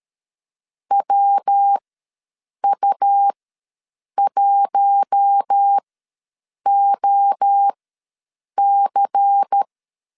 Morse code, heard as a keyed tone, about 13 wpm, spaced as for 7 wpm, 800 Hz. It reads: WU1OC
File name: LEZIONI / 5 LBVC / 5 LBVC ws 12wpm 3159